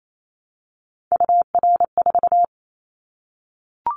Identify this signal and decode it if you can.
Morse code, heard as a keyed tone, about 28 wpm, 700 Hz. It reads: UR4
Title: Callsign Prefixes for Top 25 DXCC - Repeated 3x - 28wpm